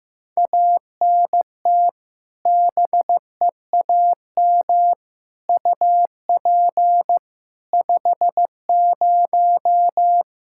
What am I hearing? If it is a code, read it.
Morse code, heard as a keyed tone, about 16 wpm, 700 Hz. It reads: ANT BEAM UP 50
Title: Practice QSO - Exchange 2 - Info - 15wpm